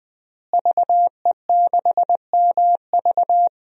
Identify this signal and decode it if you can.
Morse code, heard as a keyed tone, about 21 wpm, 700 Hz. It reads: VE6MV
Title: DXCC-100 Call Signs - 5 Characters - Repeated 3x - 20wpm